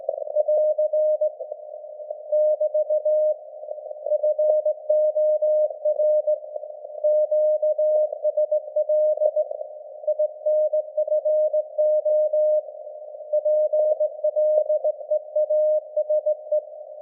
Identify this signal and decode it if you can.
Morse code, heard as a keyed tone, about 19 wpm, 610 Hz. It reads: <AR> <BT> FOR QSL INFO PLEASE
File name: laengstwellenempfang / SDRSharp 17kHz AF